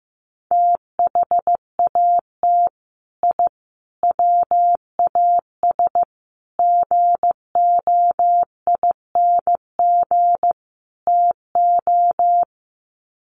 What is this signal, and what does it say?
Morse code, heard as a keyed tone, about 15 wpm, 700 Hz. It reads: THAT I WAS GOING TO
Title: Sets of 5 Words Encore 15wpm